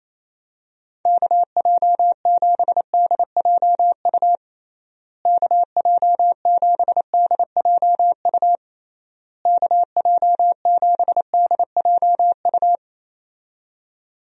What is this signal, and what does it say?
Morse code, heard as a keyed tone, about 27 wpm, 700 Hz. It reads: KJ7DJU KJ7DJU KJ7DJU
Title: USA Call Signs - 6 Characters - Repeated 3x - 28wpm